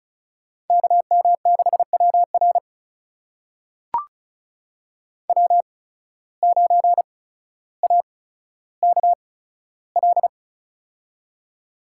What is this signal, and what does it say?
Morse code, heard as a keyed tone, about 36 wpm, spaced as for 15 wpm, 700 Hz. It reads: KM6WR W9AKL
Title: USA Call Signs - 5 Characters - 8x Character Spacing - 35wpm